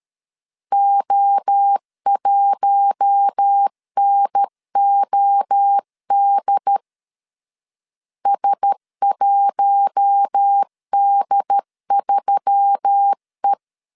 Morse code, heard as a keyed tone, about 13 wpm, 800 Hz. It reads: O1NOD S1D3E
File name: CW / DET / 1 6 1500C S C 12wpm 3533